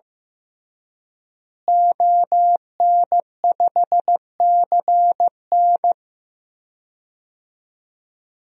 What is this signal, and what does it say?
Morse code, heard as a keyed tone, about 15 wpm, 700 Hz. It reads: ON5CN
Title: DXCC-100 Call Signs - 5 Characters - Repeated 3x - 15wpm